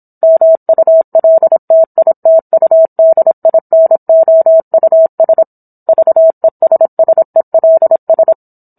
Morse code, heard as a keyed tone, about 26 wpm, 650 Hz. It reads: MULTITUDINOUS VESSELS